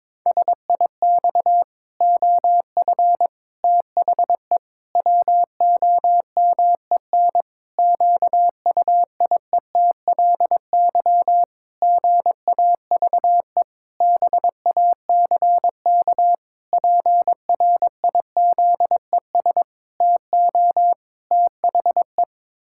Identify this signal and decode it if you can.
Morse code, heard as a keyed tone, about 22 wpm, 700 Hz. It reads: SIX OF THE WOMEN QUIETLY GAVE BACK PRIZES TO THE
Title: Pangram Challenge v2 - 22wpm